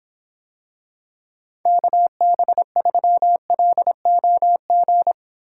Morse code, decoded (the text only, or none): KB3LOG